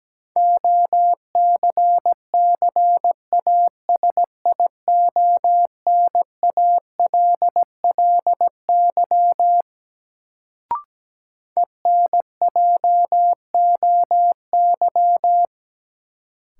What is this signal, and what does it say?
Morse code, heard as a keyed tone, about 17 wpm, 700 Hz. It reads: OCCASIONALLY ENJOY
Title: Sentences from Top 1300 Words - New Word Drill Alt - 17wpm